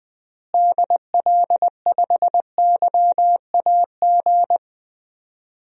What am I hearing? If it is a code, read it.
Morse code, heard as a keyed tone, about 19 wpm, 700 Hz. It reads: DL5YAG
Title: DXCC-100 Call Signs - 6 Characters - Repeated 3x - 20wpm